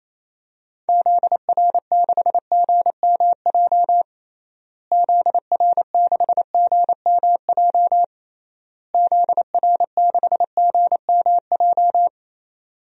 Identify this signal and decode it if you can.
Morse code, heard as a keyed tone, about 28 wpm, 700 Hz. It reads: ZR6GMJ ZR6GMJ ZR6GMJ
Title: DXCC-100 Call Signs - 6 Characters - Repeated 3x - 28wpm